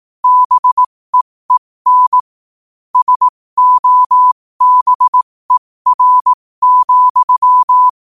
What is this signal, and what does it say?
Morse code, heard as a keyed tone, about 18 wpm, spaced as for 15 wpm, 1000 Hz. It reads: BEEN SOBER,
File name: qotdfiles / 15wpm